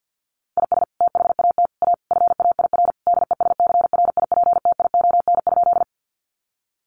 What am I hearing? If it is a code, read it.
Morse code, heard as a keyed tone, about 50 wpm, 700 Hz. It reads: IS THAT A FAIR DESCRIPTION?
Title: Sentences from Top 1300 Words 50wpm